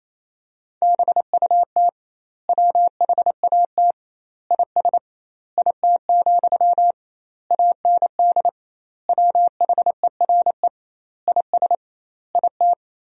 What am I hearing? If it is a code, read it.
Morse code, heard as a keyed tone, about 28 wpm, 700 Hz. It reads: BUT WHAT IS IT, AND WHERE IS IT
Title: 2x Word-Spacing - Sentences from Top 300 Words - 28wpm